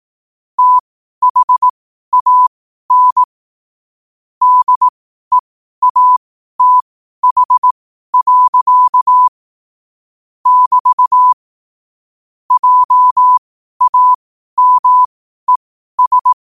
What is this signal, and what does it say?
Morse code, heard as a keyed tone, about 18 wpm, spaced as for 12 wpm, 1000 Hz. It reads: THAN DEATH. <BT> JAMES
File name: qotdfiles / 12wpm